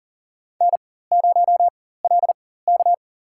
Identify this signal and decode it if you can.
Morse code, heard as a keyed tone, about 40 wpm, spaced as for 19 wpm, 700 Hz. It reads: N0LK
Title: USA Call Signs - 4x Character Spacing - 40wpm